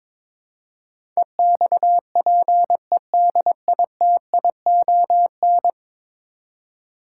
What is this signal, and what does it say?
Morse code, heard as a keyed tone, about 22 wpm, 700 Hz. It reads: EXPEDITION
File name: practice / 22wpm / 10 Letter Words - First 3 Letters Given - 22wpm